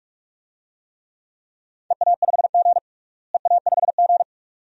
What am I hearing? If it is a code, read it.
Morse code, heard as a keyed tone, about 41 wpm, 700 Hz. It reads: EA5G EA5G
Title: DXCC-100 Call Signs - 4 Characters - Repeated 2x - 45wpm